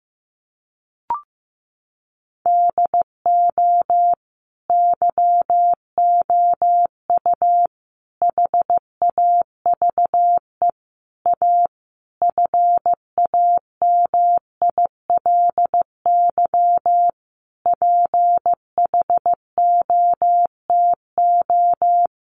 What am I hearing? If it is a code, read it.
Morse code, heard as a keyed tone, about 14 wpm, 700 Hz. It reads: DO YOU HAVE A FAMILY PHOTO
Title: Sentences from Top 1200 Words 15wpm